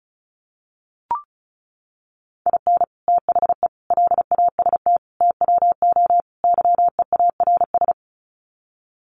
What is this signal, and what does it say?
Morse code, heard as a keyed tone, about 34 wpm, 700 Hz. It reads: IN THE LAST TWO YEARS
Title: Sets of 5 Words 35wpm